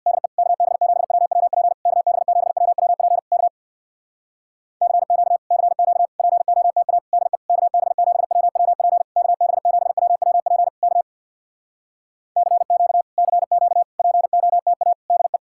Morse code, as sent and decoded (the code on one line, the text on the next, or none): -.. / -.- -... --... .-- .-- .--. / -.- -... --... .-- .-- .--. / -.- / -.-. --.- / -.-. --.- / .--. --- - .- / -.. / -.- -... --... .-- .-- .--. / -.- -... --... .-- .-- .--. / -.- / -.-. --.- / -.-. --.- / .--. --- - .- / -.. .
DE KB7WWP KB7WWP K CQ CQ POTA DE KB7WWP KB7WWP K CQ CQ POTA DE